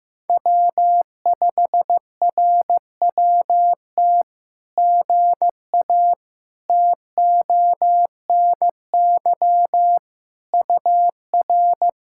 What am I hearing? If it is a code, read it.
Morse code, heard as a keyed tone, about 17 wpm, 700 Hz. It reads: W5RWT GA TONY UR
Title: SOTA QSO Practice 15wpm